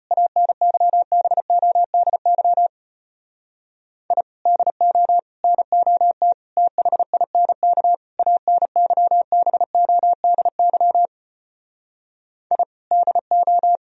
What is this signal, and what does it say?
Morse code, heard as a keyed tone, about 33 wpm, 700 Hz. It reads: ANYBODY I DO NOT THINK ANYBODY I DO